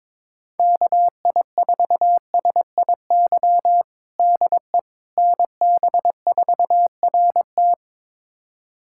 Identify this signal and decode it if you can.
Morse code, heard as a keyed tone, about 22 wpm, 700 Hz. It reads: KI4SIY DE NB4RT
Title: USA Callsign DE Callsign 22wpm